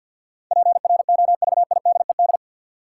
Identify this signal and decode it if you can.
Morse code, heard as a keyed tone, about 51 wpm, 700 Hz. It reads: PROVIDED
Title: Sentences from Top 1500 Words - New Word Drill Alt - 50wpm